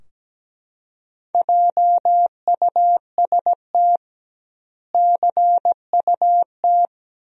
Morse code, heard as a keyed tone, about 17 wpm, 700 Hz. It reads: JUST CUT
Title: 2x Word-Spacing - Sentences from Top 300 Words - 17wpm